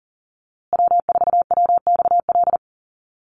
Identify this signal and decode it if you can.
Morse code, heard as a keyed tone, about 40 wpm, 700 Hz. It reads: W4WXL
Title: USA Call Signs Repeated 2x 40wpm